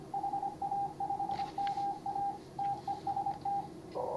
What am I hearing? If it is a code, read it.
Morse code, heard as a keyed tone, about 24 wpm, 810 Hz, with radio noise. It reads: HA5MA TEST